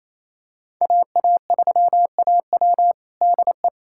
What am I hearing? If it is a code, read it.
Morse code, heard as a keyed tone, about 28 wpm, 700 Hz. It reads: AA3AW DE
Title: USA Callsign DE Callsign - 5 Characters - 28wpm